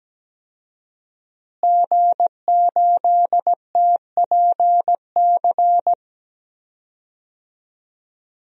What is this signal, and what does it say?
Morse code, heard as a keyed tone, about 17 wpm, 700 Hz. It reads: G8TPC
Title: DXCC-100 Call Signs - 5 Characters - Repeated 3x - 17wpm